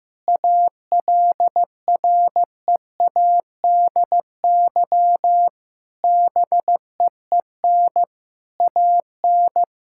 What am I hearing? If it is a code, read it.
Morse code, heard as a keyed tone, about 15 wpm, 700 Hz. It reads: ALREADY BEEN AN
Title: Sentences from Top 1000 Words - Review Part 6 - 15wpm